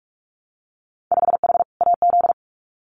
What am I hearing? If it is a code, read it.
Morse code, heard as a keyed tone, about 46 wpm, 690 Hz. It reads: 5H AZ
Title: Winter Field Day Exchanges Encore 45wpm